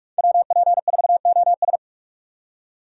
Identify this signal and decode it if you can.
Morse code, heard as a keyed tone, about 46 wpm, 700 Hz. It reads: WP4OS
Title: USA Call Signs - 5 Characters - 3x Character Spacing - 45wpm